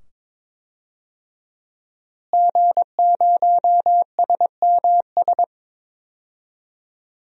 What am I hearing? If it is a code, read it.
Morse code, heard as a keyed tone, about 22 wpm, 700 Hz. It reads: G0SMS
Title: DXCC-100 Call Signs - 5 Characters - Repeated 2x - 22wpm